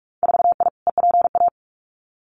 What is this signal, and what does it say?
Morse code, heard as a keyed tone, about 44 wpm, 700 Hz. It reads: 4I EPA
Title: Winter Field Day Exchanges 45wpm